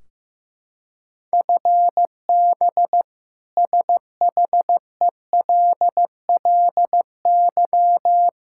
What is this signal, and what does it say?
Morse code, heard as a keyed tone, about 15 wpm, 700 Hz. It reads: FB SHELLY
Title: K1USN SST QSO Practice 15wpm